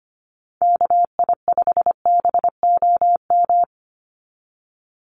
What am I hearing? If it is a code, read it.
Morse code, heard as a keyed tone, about 25 wpm, 700 Hz. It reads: KI5BOM
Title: USA Call Signs 25wpm